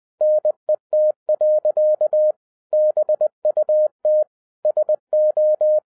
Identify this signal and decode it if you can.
Morse code, heard as a keyed tone, about 20 wpm, 610 Hz. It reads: NET. BUT SO
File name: 2012-01-mp3 / n16-THROUGH EIGHT MONTHS-20wpm